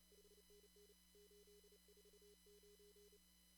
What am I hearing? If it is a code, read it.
Morse code, heard as a keyed tone, about 25 wpm, 410 Hz, with noise. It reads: 5NN 849
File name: CW / kt5j wpx cw-op-k5pi